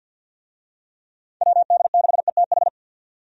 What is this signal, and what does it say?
Morse code, heard as a keyed tone, about 52 wpm, 700 Hz. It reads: WD6ETH
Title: USA Call Signs - 6 Characters - 50wpm